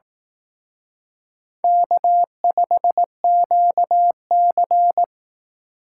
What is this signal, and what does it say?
Morse code, heard as a keyed tone, about 18 wpm, 700 Hz. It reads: K5QC
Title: Speed Racing - USA Call Signs 15wpm